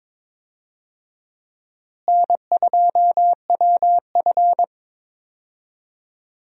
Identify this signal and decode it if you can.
Morse code, heard as a keyed tone, about 22 wpm, 700 Hz. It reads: N2WF
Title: USA Call Signs - 4 Characters - Repeated 2x - 22wpm